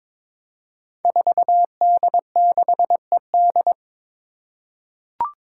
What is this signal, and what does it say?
Morse code, heard as a keyed tone, about 22 wpm, 700 Hz. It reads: 4D6ED
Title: Five Challenging Letter-Number Combinations 22wpm